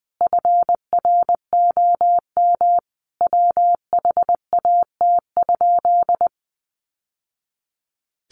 Morse code, heard as a keyed tone, about 20 wpm, 700 Hz. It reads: FROM WHAT?
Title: Sentences from Top 1100 Words 20wpm